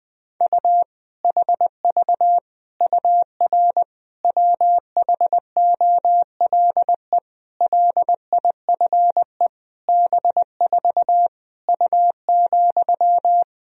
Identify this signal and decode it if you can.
Morse code, heard as a keyed tone, about 20 wpm, 700 Hz. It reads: U HV UR WHOLE LIFE B4 U,
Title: Sentences from Top 400 Words and Abbreviations 20wpm